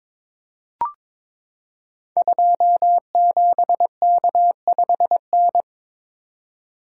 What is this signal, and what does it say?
Morse code, heard as a keyed tone, about 22 wpm, 700 Hz. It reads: 27K5N
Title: Code Groups - 5 Letter:Numbers - 22wpm